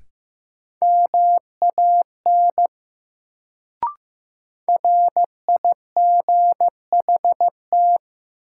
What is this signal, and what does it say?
Morse code, heard as a keyed tone, about 15 wpm, 700 Hz. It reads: MAN RIGHT
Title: Sets of 2 Words - New Word Drill - 15wpm